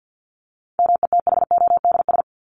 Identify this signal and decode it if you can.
Morse code, heard as a keyed tone, about 48 wpm, 700 Hz. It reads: METHODS
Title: ES:S Suffix - Top 100 Words with Base Words 50wpm